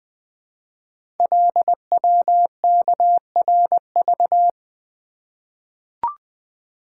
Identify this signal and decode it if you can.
Morse code, heard as a keyed tone, about 20 wpm, 700 Hz. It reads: LWKRV
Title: Code Groups - 5 Letters - 20wpm